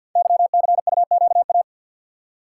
Morse code, heard as a keyed tone, about 47 wpm, 700 Hz. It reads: YCUQA
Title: Code Groups - 5 Letters - 50wpm